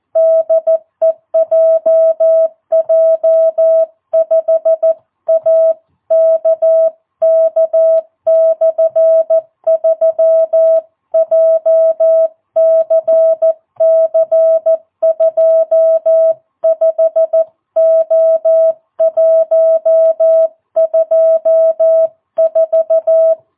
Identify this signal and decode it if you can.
Morse code, heard as a keyed tone, about 14 wpm, 660 Hz. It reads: DE JJ5AKK/3 JCC25O124
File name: jh8cbh / 2022cw